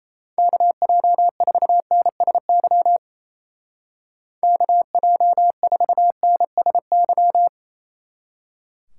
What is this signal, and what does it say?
Morse code, heard as a keyed tone, about 35 wpm, 700 Hz. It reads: KJ4NSY KJ4NSY